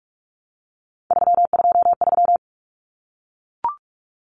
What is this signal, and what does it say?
Morse code, heard as a keyed tone, about 44 wpm, 700 Hz. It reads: 323
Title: Common Cut Numbers - 3-Digit 45wpm